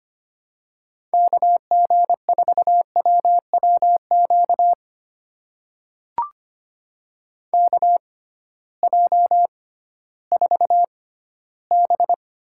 Morse code, heard as a keyed tone, about 25 wpm, spaced as for 14 wpm, 700 Hz. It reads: KG4WWQ KJ4B
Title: USA Call Signs - 6 Characters - 6x Character Spacing - 25wpm